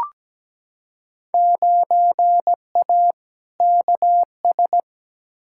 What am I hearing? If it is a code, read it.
Morse code, heard as a keyed tone, about 17 wpm, 700 Hz. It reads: 9A KS
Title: Field Day Exchanges Encore - 17wpm